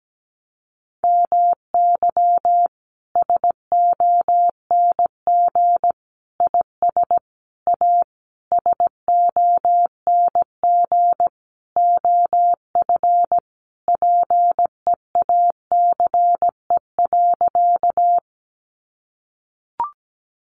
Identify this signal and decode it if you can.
Morse code, heard as a keyed tone, about 17 wpm, 700 Hz. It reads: MY SONG IS A SONG OF PEACE.
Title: Sentences from Top 1500 Words 17wpm